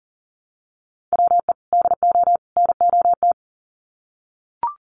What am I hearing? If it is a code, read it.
Morse code, heard as a keyed tone, about 40 wpm, 700 Hz. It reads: WE DO NOT
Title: Sets of 3 Words 40wpm